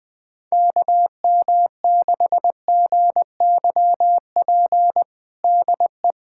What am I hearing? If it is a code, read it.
Morse code, heard as a keyed tone, about 21 wpm, 700 Hz. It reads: KM6GYP DE
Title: USA Callsign DE Callsign - 6 Characters - 20wpm